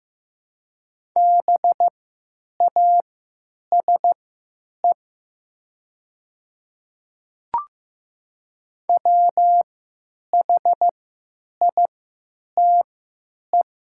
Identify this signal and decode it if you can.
Morse code, heard as a keyed tone, about 15 wpm, spaced as for 9 wpm, 700 Hz. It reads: BASE WHITE
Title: 3x Character Spacing - Top 500 Words - 15wpm